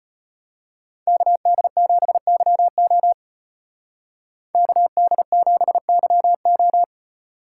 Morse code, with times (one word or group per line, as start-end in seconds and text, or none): 1.1-3.1 KD7YO
4.5-6.8 KD7YO